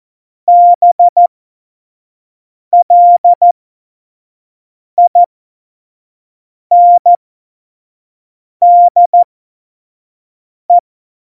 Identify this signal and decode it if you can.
Morse code, heard as a keyed tone, about 14 wpm, spaced as for 5 wpm, 700 Hz. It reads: BLINDE
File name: Morsekurs / Witze / Witz 24. November 2020, 25 F70